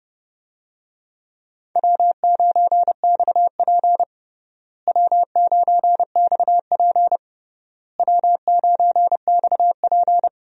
Morse code, heard as a keyed tone, about 30 wpm, 700 Hz. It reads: W9XP W9XP W9XP
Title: USA Call Signs - 4 Characters - Repeated 3x - 30wpm